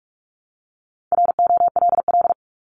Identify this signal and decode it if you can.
Morse code, heard as a keyed tone, about 46 wpm, 700 Hz. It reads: ROLL
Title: Multi-letter Phonemes Repeated 3x 45wpm